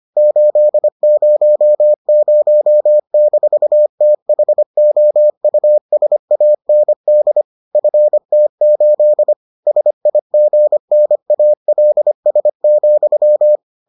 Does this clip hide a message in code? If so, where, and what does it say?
Yes: Morse code from 0.2 to 13.6 s, heard as a keyed tone, about 25 wpm, 600 Hz. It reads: 800-THOUSAND FT8 SIGNALS,